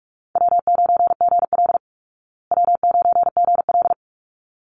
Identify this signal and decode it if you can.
Morse code, heard as a keyed tone, about 44 wpm, 700 Hz. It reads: W9GL W9GL
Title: Call Signs Repeated 3x - 45wpm